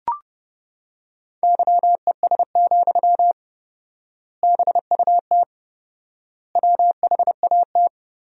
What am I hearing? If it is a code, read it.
Morse code, heard as a keyed tone, about 29 wpm, 700 Hz. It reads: YES, BUT WHAT
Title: 4x Word-Spacing - Sentences from Top 500 Words - 30wpm